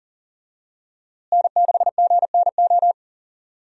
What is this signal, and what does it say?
Morse code, heard as a keyed tone, about 40 wpm, 700 Hz. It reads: N6GNO
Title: USA Call Signs - 5 Characters - Repeated 3x - 40wpm